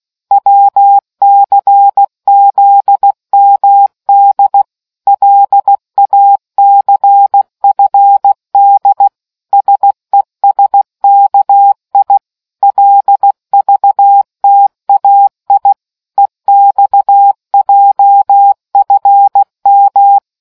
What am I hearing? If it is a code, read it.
Morse code, heard as a keyed tone, about 16 wpm, 800 Hz. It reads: WCZMD LACFD SESKI LVTAI EXJFM